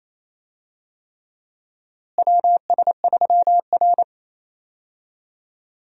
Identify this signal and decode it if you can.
Morse code, heard as a keyed tone, about 28 wpm, 700 Hz. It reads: WS3R
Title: USA Call Signs - 4 Characters - Repeated 3x - 28wpm